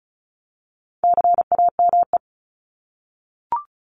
Morse code, heard as a keyed tone, about 34 wpm, 700 Hz. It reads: CAME
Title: Top 200 Words Repeated 35wpm